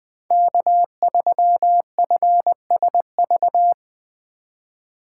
Morse code, heard as a keyed tone, about 21 wpm, 700 Hz. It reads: K3FSV
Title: USA Call Signs - 2x Character Spacing - 20wpm